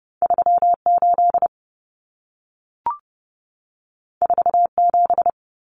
Morse code, heard as a keyed tone, about 30 wpm, 700 Hz. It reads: 38 47
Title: Numbers - 2-Digit 30wpm